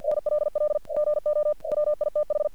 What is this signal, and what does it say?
Morse code, heard as a keyed tone, about 48 wpm, 600 Hz. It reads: NLL GO WITH
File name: archived / CRSnet